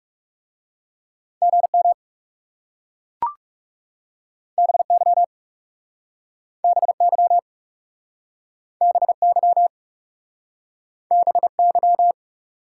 Morse code, heard as a keyed tone, about 40 wpm, 700 Hz. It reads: GM BY BY BY BY